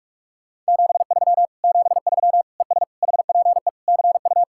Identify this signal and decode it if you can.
Morse code, heard as a keyed tone, about 57 wpm, 700 Hz. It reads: 73 73 ES HPE CU